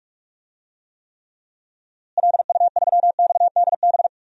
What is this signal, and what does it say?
Morse code, heard as a keyed tone, about 46 wpm, 700 Hz. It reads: LU3XDB
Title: DXCC-100 Call Signs - 6 Characters - Repeated 3x - 45wpm